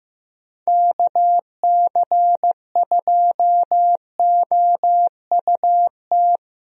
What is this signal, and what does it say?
Morse code, heard as a keyed tone, about 15 wpm, 700 Hz. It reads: KC2OUT
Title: USA Call Signs - 6 Characters - 15wpm